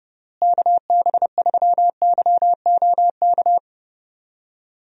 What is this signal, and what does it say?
Morse code, heard as a keyed tone, about 29 wpm, 700 Hz. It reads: KB3YOK
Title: USA Call Signs - 6 Characters - Repeated 3x - 30wpm